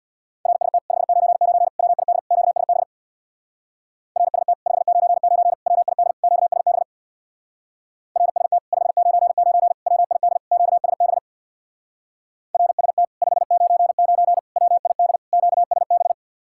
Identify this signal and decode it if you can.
Morse code, heard as a keyed tone, about 62 wpm, 700 Hz. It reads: RST 599 WID QSB RST 599 WID QSB RST 599 WID QSB RST 599 WID QSB
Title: Speed-Racing Common QSO Phrases Encore - 50wpm